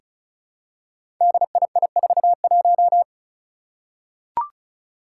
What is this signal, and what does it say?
Morse code, heard as a keyed tone, about 36 wpm, 700 Hz. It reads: DII41
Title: Five Challenging Letter-Number Combinations 35wpm